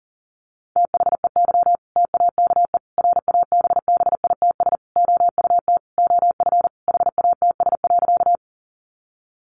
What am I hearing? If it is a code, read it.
Morse code, heard as a keyed tone, about 40 wpm, 700 Hz. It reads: THEY TAKE RABBITS OUT OF HATS.
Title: Easy Phrases 40wpm